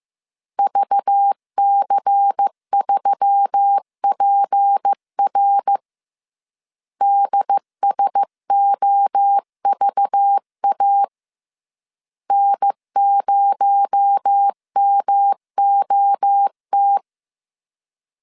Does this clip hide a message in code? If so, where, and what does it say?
Yes: Morse code from 0.6 to 17.0 s, heard as a keyed tone, about 15 wpm, 800 Hz. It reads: VC3PR DSOVA N0MOT